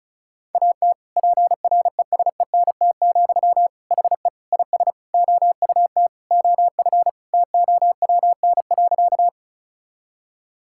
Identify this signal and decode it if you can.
Morse code, heard as a keyed tone, about 34 wpm, 700 Hz. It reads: AT PRESENT, HE IS OUT OF TOWN.
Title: Sentences from Top 1000 Words - Review Part 5 - 35wpm